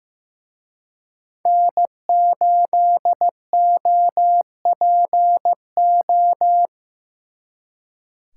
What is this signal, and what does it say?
Morse code, heard as a keyed tone, about 18 wpm, 700 Hz. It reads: N8OPO
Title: Speed Racing - USA 1x3 Call Signs 15wpm